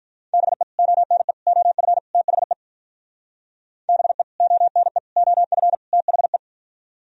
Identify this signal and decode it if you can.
Morse code, heard as a keyed tone, about 45 wpm, 700 Hz. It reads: BE ONE OF THE BE ONE OF THE